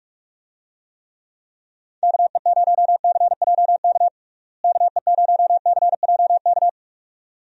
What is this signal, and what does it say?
Morse code, heard as a keyed tone, about 46 wpm, 700 Hz. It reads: KE0CJK KE0CJK
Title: USA Call Signs - 6 Characters - Repeated 2x - 45wpm